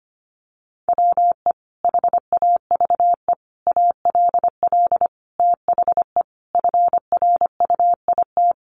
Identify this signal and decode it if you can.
Morse code, heard as a keyed tone, about 25 wpm, 700 Hz. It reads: WE HAVE ALL THE FRUIT
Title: Sentences from Top 1300 Words 25wpm